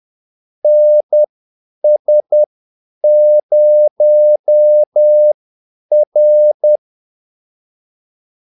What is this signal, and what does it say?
Morse code, heard as a keyed tone, about 10 wpm, spaced as for 8 wpm, 600 Hz. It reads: NS0R
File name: Callsigns-10-8wpm